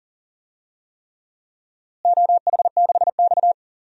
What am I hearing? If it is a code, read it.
Morse code, heard as a keyed tone, about 40 wpm, 700 Hz. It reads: OH6X
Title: DXCC-100 Call Signs - 4 Characters - Repeated 3x - 40wpm